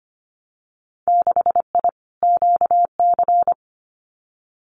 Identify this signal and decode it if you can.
Morse code, heard as a keyed tone, about 25 wpm, 690 Hz. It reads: 6I QC
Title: Winter Field Day Exchanges Encore 25wpm